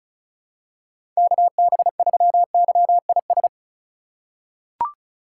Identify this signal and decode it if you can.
Morse code, heard as a keyed tone, about 34 wpm, 700 Hz. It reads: KB3YIS
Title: USA Call Signs - 6 Characters - 4x Character Spacing - 35wpm